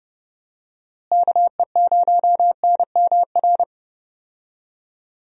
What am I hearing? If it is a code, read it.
Morse code, heard as a keyed tone, about 30 wpm, 700 Hz. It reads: KE0NMR
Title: USA Call Signs - 6 Characters - Repeated 2x - 30wpm